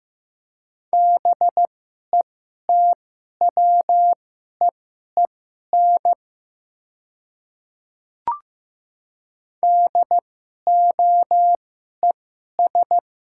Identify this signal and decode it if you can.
Morse code, heard as a keyed tone, about 15 wpm, spaced as for 11 wpm, 700 Hz. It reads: BETWEEN DOES
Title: 2x Character Spacing - Top 300 Words - 15wpm